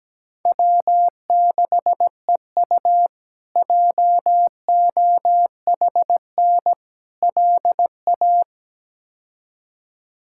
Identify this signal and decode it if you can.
Morse code, heard as a keyed tone, about 17 wpm, 700 Hz. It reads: W6EU JOHN LA
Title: K1USN SST QSO Practice - 17wpm